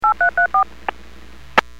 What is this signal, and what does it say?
DTMF, heard as telephone touch tones, tones of about 100 ms. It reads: 8334
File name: WPS / 00220 18-01 WPS 451.25 WPS